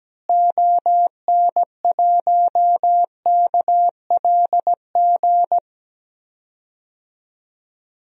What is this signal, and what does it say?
Morse code, heard as a keyed tone, about 17 wpm, 700 Hz. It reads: ON1KLG
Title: DXCC-100 Call Signs - 6 Characters - Repeated 3x - 17wpm